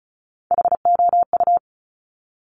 Morse code, heard as a keyed tone, about 36 wpm, 700 Hz. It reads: HOU
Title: Top Three Letter Combinations in Words 35wpm